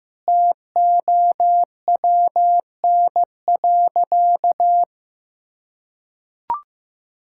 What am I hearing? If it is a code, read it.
Morse code, heard as a keyed tone, about 15 wpm, 700 Hz. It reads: TOWN.
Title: Sentences from Top 1600 Words 15wpm